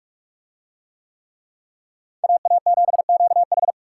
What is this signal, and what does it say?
Morse code, heard as a keyed tone, about 44 wpm, 700 Hz. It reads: AA7QH